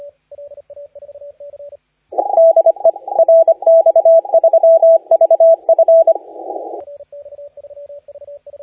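Morse code, heard as a keyed tone, about 25 wpm, 660 Hz, with radio noise. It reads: DE RX3VF
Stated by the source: Morse contest audio recorded by LA4C during the SAC CW 2018 contest.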